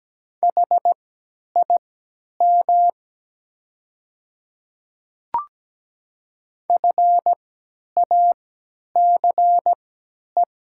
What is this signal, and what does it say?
Morse code, heard as a keyed tone, about 17 wpm, spaced as for 10 wpm, 700 Hz. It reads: HIM FACE
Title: 3x Character Spacing - Top 300 Words - 17wpm